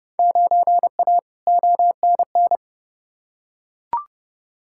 Morse code, heard as a keyed tone, about 30 wpm, 700 Hz. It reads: 9A ONN
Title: Field Day Exchanges Encore - 30wpm